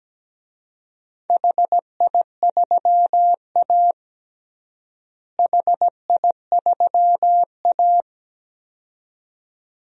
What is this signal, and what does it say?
Morse code, heard as a keyed tone, about 17 wpm, 700 Hz. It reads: HI3A HI3A
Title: DXCC-100 Call Signs - 4 Characters - Repeated 2x - 17wpm